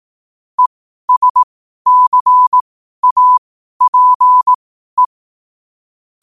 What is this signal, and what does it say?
Morse code, heard as a keyed tone, about 18 wpm, spaced as for 12 wpm, 1000 Hz. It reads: ESCAPE
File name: qotdfiles / 12wpm